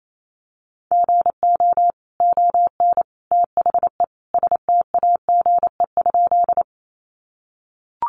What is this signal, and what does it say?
Morse code, heard as a keyed tone, about 28 wpm, 700 Hz. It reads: GO ON THE STAGE?
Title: Sentences from Top 1200 Words 28wpm